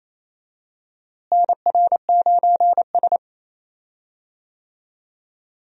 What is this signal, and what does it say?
Morse code, heard as a keyed tone, about 28 wpm, 700 Hz. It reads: NR9S
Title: USA Call Signs - 4 Characters - Repeated 2x - 28wpm